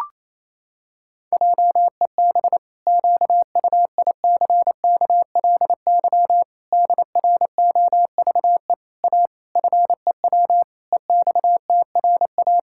Morse code, heard as a keyed tone, about 28 wpm, 700 Hz. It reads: JEB QUICKLY DROVE A FEW EXTRA
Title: Pangram Challenge v2 - 28wpm